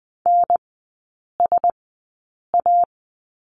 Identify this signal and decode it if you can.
Morse code, heard as a keyed tone, about 20 wpm, spaced as for 8 wpm, 700 Hz. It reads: NSA
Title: 035 Lesson 7e - Letter s - Three Characters 20wpm